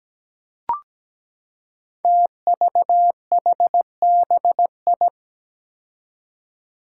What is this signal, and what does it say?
Morse code, heard as a keyed tone, about 17 wpm, 700 Hz. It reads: TVHBI
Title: Five Challenging Letter-Number Combinations 17wpm